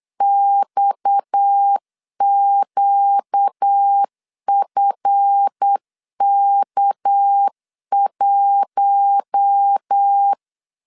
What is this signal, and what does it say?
Morse code, heard as a keyed tone, about 8 wpm, 800 Hz. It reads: XQFK1
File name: LEZIONI / 8 YJ45 / 8 Y 8wpm 3245